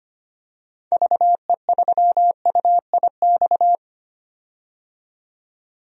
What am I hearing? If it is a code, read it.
Morse code, heard as a keyed tone, about 25 wpm, 700 Hz. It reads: VE3UIX
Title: DXCC-100 Call Signs - 6 Characters - 25wpm